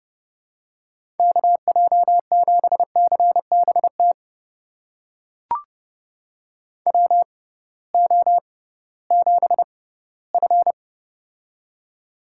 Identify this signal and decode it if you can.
Morse code, heard as a keyed tone, about 30 wpm, spaced as for 17 wpm, 700 Hz. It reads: KJ7CBT WO7F
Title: USA Call Signs - 6x Character Spacing - 30wpm